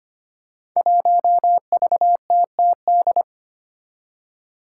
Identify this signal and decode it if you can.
Morse code, heard as a keyed tone, about 25 wpm, 700 Hz. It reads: 1VTTD
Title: Five Challenging Letter-Number Combinations 25wpm